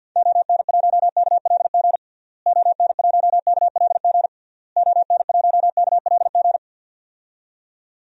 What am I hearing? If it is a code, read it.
Morse code, heard as a keyed tone, about 52 wpm, 700 Hz. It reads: ON1KLG ON1KLG ON1KLG
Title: DXCC-100 Call Signs - 6 Characters - Repeated 3x - 50wpm